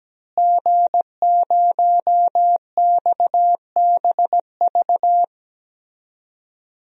Morse code, heard as a keyed tone, about 17 wpm, 700 Hz. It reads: G0XBV
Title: DXCC-100 Call Signs - 5 Characters - Repeated 3x - 17wpm